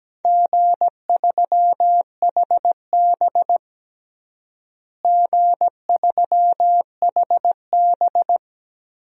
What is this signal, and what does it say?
Morse code, heard as a keyed tone, about 17 wpm, 700 Hz. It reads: G3HB G3HB
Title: DXCC-100 Call Signs - 4 Characters - Repeated 3x - 17wpm